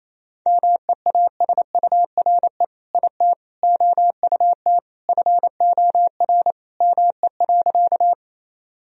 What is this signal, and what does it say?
Morse code, heard as a keyed tone, about 27 wpm, 700 Hz. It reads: MEASURE IT OUT FOR ME.
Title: Sentences from Top 1000 Words - Review Part 2 - 28wpm